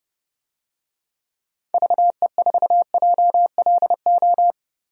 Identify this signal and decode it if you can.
Morse code, heard as a keyed tone, about 31 wpm, 700 Hz. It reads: VE4JLO
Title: DXCC-100 Call Signs - 6 Characters - Repeated 2x - 30wpm